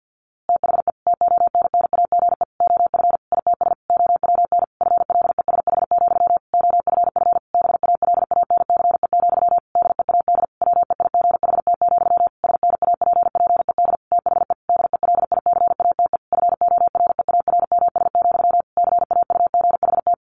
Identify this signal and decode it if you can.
Morse code, heard as a keyed tone, about 48 wpm, 700 Hz. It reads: THE TONNAGE OF ITS OWN FLESH, OFF BALANCE, DEAD WEIGHT, SNAPPED THE DELICATE FOREARMS, CAUGHT
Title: A Sound Of Thunder 50wpm